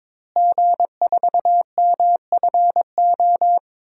Morse code, heard as a keyed tone, about 21 wpm, 700 Hz. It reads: G4MFO
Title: DXCC-100 Call Signs - 5 Characters - Repeated 2x - 22wpm